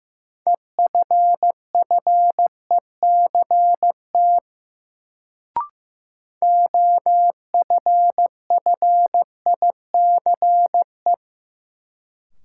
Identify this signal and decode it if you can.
Morse code, heard as a keyed tone, about 16 wpm, 700 Hz. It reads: EFFECT OFFICE
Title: Sentences from Top 1000 Words - New Word Drill - 15wpm